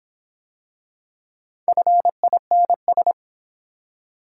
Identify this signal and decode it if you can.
Morse code, heard as a keyed tone, about 26 wpm, 700 Hz. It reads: FINS